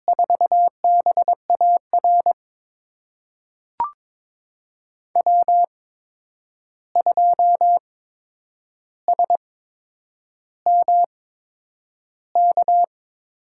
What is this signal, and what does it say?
Morse code, heard as a keyed tone, about 22 wpm, spaced as for 9 wpm, 700 Hz. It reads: 4BAR W2SMK
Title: USA Call Signs - 5 Characters - 8x Character Spacing - 22wpm